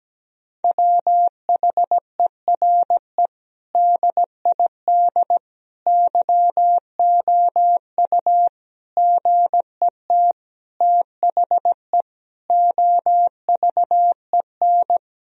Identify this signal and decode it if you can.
Morse code, heard as a keyed tone, about 17 wpm, 700 Hz. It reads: WHERE DID YOU GET THE OVEN
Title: Sentences from Top 1000 Words - Review Part 4 - 17wpm